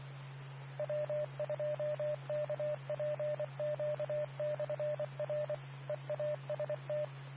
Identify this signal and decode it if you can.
Morse code, heard as a keyed tone, about 24 wpm, 600 Hz, with radio noise. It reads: W2KPQ/R EAST
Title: Nassau and Suffolk County Amateur Repeaters